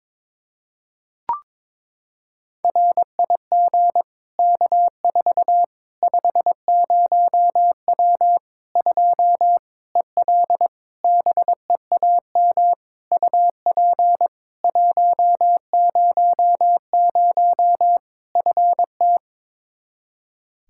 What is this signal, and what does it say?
Morse code, heard as a keyed tone, about 22 wpm, 700 Hz. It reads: RIG K4 50W 2 EL BEAM UP 100 FT
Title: Practice QSO - Exchange 2 - Info - 22wpm